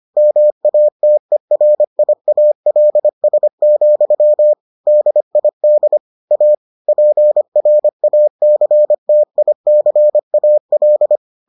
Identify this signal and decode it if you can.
Morse code, heard as a keyed tone, about 25 wpm, 600 Hz. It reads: MATERIALS, DID A PRACTICAL